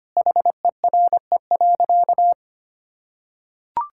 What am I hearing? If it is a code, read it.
Morse code, heard as a keyed tone, about 25 wpm, 700 Hz. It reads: HERE.
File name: practice / 25wpm / Sentences with 4-Word Phrases - 25wpm